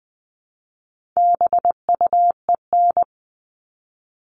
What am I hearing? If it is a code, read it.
Morse code, heard as a keyed tone, about 20 wpm, 700 Hz. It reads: BUEN
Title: All Cut Numbers - 4-Digit 20wpm